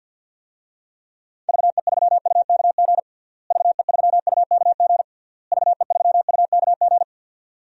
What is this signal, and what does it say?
Morse code, heard as a keyed tone, about 51 wpm, 700 Hz. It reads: VE3UKG VE3UKG VE3UKG
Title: DXCC-100 Call Signs - 6 Characters - Repeated 3x - 50wpm